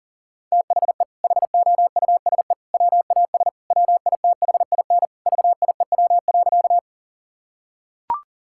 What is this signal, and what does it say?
Morse code, heard as a keyed tone, about 40 wpm, 700 Hz. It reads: THE HOUSE WAS WITHIN VIEW.
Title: Sentences from Top 1000 Words - Review Part 6 - 40wpm